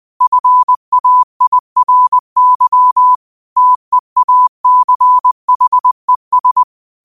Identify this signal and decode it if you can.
Morse code, heard as a keyed tone, about 20 wpm, 1000 Hz. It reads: FAIRY TEACHES